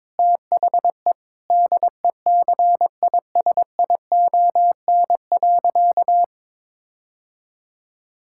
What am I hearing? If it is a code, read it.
Morse code, heard as a keyed tone, about 21 wpm, 700 Hz. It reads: THE DECISION.
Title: Sentences from Top 1600 Words 22wpm